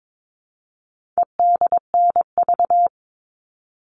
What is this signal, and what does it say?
Morse code, heard as a keyed tone, about 22 wpm, 700 Hz. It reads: EDNV
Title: All Cut Numbers - 4-Digit 22wpm